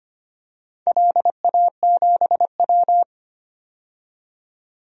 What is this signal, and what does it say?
Morse code, heard as a keyed tone, about 25 wpm, 700 Hz. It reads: LA7W
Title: DXCC-100 Call Signs - 4 Characters - 25wpm